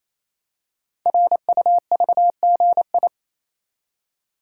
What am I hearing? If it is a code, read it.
Morse code, heard as a keyed tone, about 27 wpm, 700 Hz. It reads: RUVGI
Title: Code Groups - 5 Letter:Numbers - 28wpm